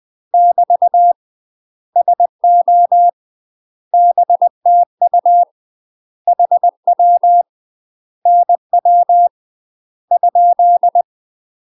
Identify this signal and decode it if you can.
Morse code, heard as a keyed tone, about 20 wpm, 700 Hz. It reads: <BT> SO BTU HW NW ?
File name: UK MASTER QSOs TRAINING / 20WPM / UK MASTER QSOs 0031